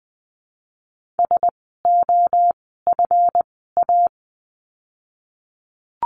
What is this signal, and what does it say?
Morse code, heard as a keyed tone, about 20 wpm, 700 Hz. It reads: S O F A
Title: 139 Lesson 23f - Letter f - Words 20wpm